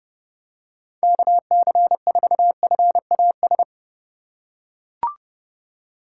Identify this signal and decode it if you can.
Morse code, heard as a keyed tone, about 30 wpm, 700 Hz. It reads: KC4FAS
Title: USA Call Signs - 4x Character Spacing - 30wpm